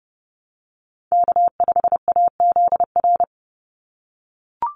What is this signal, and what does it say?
Morse code, heard as a keyed tone, about 30 wpm, 700 Hz. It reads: K5AZR
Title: Call Signs - 30wpm